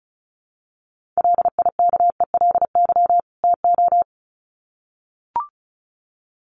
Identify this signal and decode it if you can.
Morse code, heard as a keyed tone, about 34 wpm, 700 Hz. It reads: LIKELY TO
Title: Sets of 2 Words 35wpm